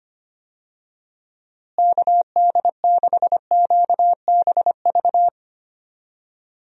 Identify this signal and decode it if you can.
Morse code, heard as a keyed tone, about 25 wpm, 700 Hz. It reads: KD6QBV
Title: USA Call Signs - 6 Characters - Repeated 2x - 25wpm